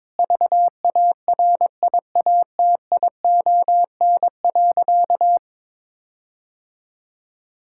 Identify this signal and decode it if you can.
Morse code, heard as a keyed tone, about 22 wpm, 700 Hz. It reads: VARIATION.
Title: Sentences from Top 1600 Words 22wpm